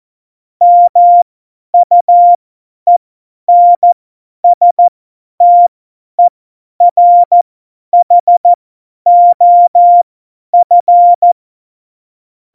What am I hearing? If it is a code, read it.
Morse code, heard as a keyed tone, about 14 wpm, spaced as for 10 wpm, 700 Hz. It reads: MUENSTERHOF
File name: Morsekurs / Witze / Witz 24. November 2020, 50 F70